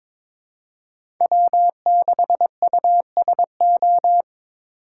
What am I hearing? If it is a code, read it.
Morse code, heard as a keyed tone, about 21 wpm, 700 Hz. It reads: W6USO
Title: USA Call Signs - 5 Characters - 3x Character Spacing - 22wpm